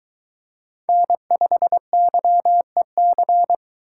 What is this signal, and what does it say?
Morse code, heard as a keyed tone, about 22 wpm, 700 Hz. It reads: N5YEC